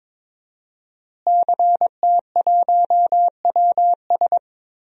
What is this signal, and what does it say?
Morse code, heard as a keyed tone, about 22 wpm, 700 Hz. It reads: CT1WS
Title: DXCC-100 Call Signs - 5 Characters - Repeated 2x - 22wpm